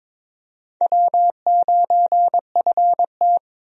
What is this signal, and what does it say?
Morse code, heard as a keyed tone, about 22 wpm, 700 Hz. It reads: W9FT
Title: USA Call Signs - 4 Characters - 22wpm